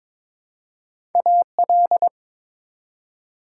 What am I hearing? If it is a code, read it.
Morse code, heard as a keyed tone, about 22 wpm, 700 Hz. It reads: AL
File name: ARRL Sections v2 - 22wpm